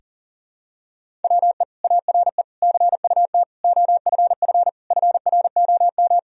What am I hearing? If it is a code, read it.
Morse code, heard as a keyed tone, about 40 wpm, 700 Hz. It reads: WE ARE CUT OFF FROM
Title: Sentences from Top 1000 Words - Review Part 1 - 40wpm